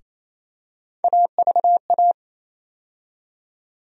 Morse code, heard as a keyed tone, about 28 wpm, 700 Hz. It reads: AVA
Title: USA Call Sign Postfixes - 3 Letters 28wpm